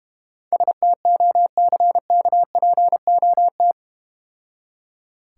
Morse code, heard as a keyed tone, about 31 wpm, 700 Hz. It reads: STOCKPOT